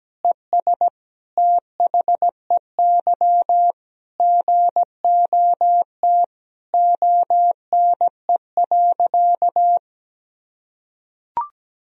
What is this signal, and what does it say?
Morse code, heard as a keyed tone, about 17 wpm, 700 Hz. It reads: ES THEY GOT ONE.
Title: Sentences from Top 300 Words and Abbreviations 17wpm